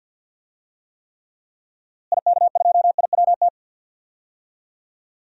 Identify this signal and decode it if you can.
Morse code, heard as a keyed tone, about 52 wpm, 700 Hz. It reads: IK2IWT
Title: DXCC-100 Call Signs - 6 Characters - Repeated 3x - 50wpm